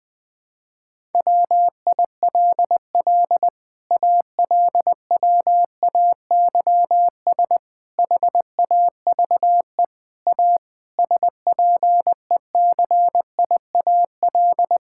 Morse code, heard as a keyed tone, about 21 wpm, 700 Hz. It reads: WILL ALWAYS HAVE A SPECIAL
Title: Sets of 9 Words - 20wpm